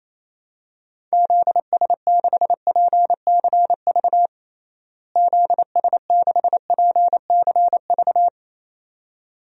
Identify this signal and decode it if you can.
Morse code, heard as a keyed tone, about 28 wpm, 700 Hz. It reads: ZS6PCV ZS6PCV
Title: DXCC-100 Call Signs - 6 Characters - Repeated 2x - 28wpm